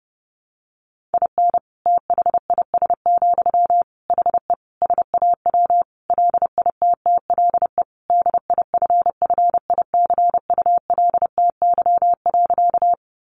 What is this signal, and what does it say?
Morse code, heard as a keyed tone, about 30 wpm, 700 Hz. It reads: IN THIS, HE SAW LITTLE DIFFICULTY.
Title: Sentences from Top 1500 Words 30wpm